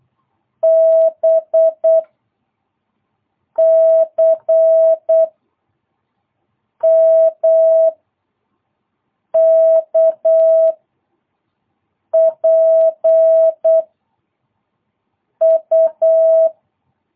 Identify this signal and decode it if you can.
Morse code, heard as a keyed tone, about 8 wpm, 660 Hz. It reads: B C M K P U
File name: jh8cbh / 2022cw